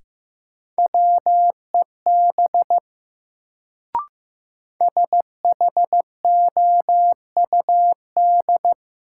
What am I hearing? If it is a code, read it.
Morse code, heard as a keyed tone, about 15 wpm, 700 Hz. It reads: WEB SHOUD
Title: Sentences from Top 1000 Words - Review Part 4 - New Word Drill - 15wpm